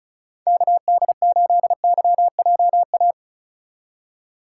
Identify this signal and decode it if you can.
Morse code, heard as a keyed tone, about 34 wpm, 700 Hz. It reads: KD8YJA
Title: USA Call Signs - 6 Characters - 4x Character Spacing - 35wpm